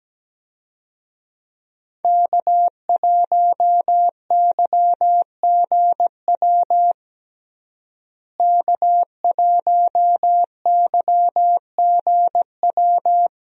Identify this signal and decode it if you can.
Morse code, heard as a keyed tone, about 17 wpm, 700 Hz. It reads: K1YGW K1YGW
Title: USA Call Signs - 5 Characters - Repeated 3x - 17wpm